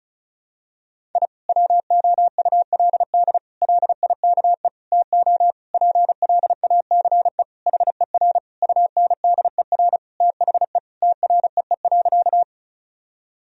Morse code, heard as a keyed tone, about 34 wpm, 700 Hz. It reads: I WOULD LIKE TO PLACE HER UNDER THE TREE.
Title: Sentences from Top 1000 Words - Review Part 1 - 35wpm